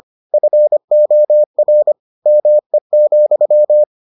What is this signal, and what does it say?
Morse code, heard as a keyed tone, about 25 wpm, 600 Hz. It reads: FOR ME,